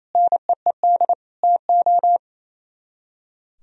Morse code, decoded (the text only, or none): NEED TO